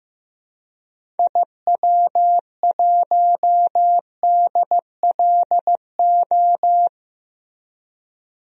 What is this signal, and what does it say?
Morse code, heard as a keyed tone, about 15 wpm, 700 Hz. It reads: IW1DLO
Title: DXCC-100 Call Signs - 6 Characters - Repeated 2x - 15wpm